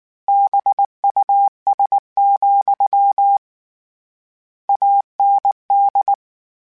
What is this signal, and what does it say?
Morse code, heard as a keyed tone, about 19 wpm, 800 Hz. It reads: BUS, AND
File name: Misc / Misc-mp3s / BIG @18